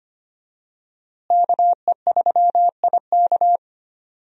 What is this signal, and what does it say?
Morse code, heard as a keyed tone, about 25 wpm, 700 Hz. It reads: KE3IK
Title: USA Call Signs - 5 Characters - Repeated 3x - 25wpm